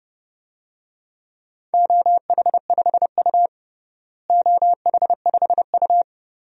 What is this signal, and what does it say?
Morse code, heard as a keyed tone, about 30 wpm, 700 Hz. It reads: OH5U OH5U
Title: DXCC-100 Call Signs - 4 Characters - Repeated 2x - 30wpm